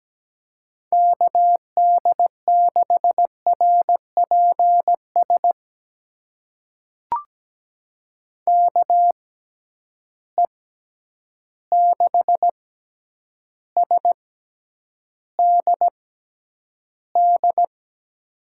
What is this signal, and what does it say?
Morse code, heard as a keyed tone, about 17 wpm, spaced as for 9 wpm, 700 Hz. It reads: KD6RPS KE6SDD
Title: USA Call Signs - 6 Characters - 6x Character Spacing - 17wpm